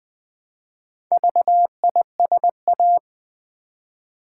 Morse code, heard as a keyed tone, about 20 wpm, 700 Hz. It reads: VISA
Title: Build A Word - 1 Letter At A Time - 3 to 5-Letter Words - 20wpm